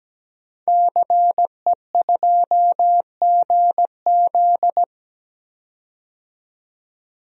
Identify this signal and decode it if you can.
Morse code, heard as a keyed tone, about 17 wpm, 700 Hz. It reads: CE2GZ
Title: DXCC-100 Call Signs - 5 Characters - 17wpm